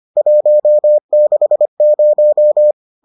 Morse code, heard as a keyed tone, about 25 wpm, 600 Hz. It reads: 160